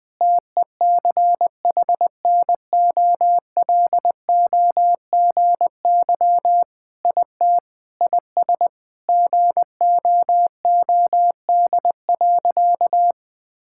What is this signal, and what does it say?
Morse code, heard as a keyed tone, about 21 wpm, 700 Hz. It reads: TECHNOLOGY IT IS GOOD.
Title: Sentences with 3-Word Phrases - Spoken Beginning - 20wpm